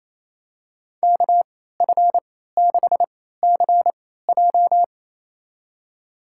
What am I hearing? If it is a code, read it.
Morse code, heard as a keyed tone, about 28 wpm, spaced as for 16 wpm, 700 Hz. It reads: KF6CJ
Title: USA Call Signs - 3x Character Spacing - 28wpm